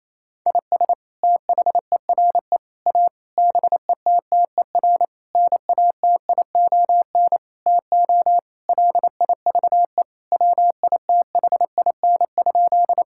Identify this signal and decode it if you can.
Morse code, heard as a keyed tone, about 28 wpm, 700 Hz. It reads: IS THERE A BETTER NATION TO LIVE WITHIN?
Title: Sentences from Top 1000 Words - Review Part 6 - 28wpm